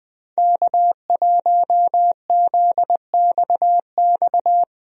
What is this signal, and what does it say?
Morse code, heard as a keyed tone, about 20 wpm, 700 Hz. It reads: K1ZXX
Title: USA Call Signs - 5 Characters - 20wpm